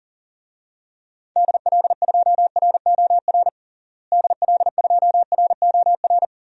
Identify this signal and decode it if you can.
Morse code, heard as a keyed tone, about 40 wpm, 700 Hz. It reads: DL2ROR DL2ROR
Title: DXCC-100 Call Signs - 6 Characters - Repeated 2x - 40wpm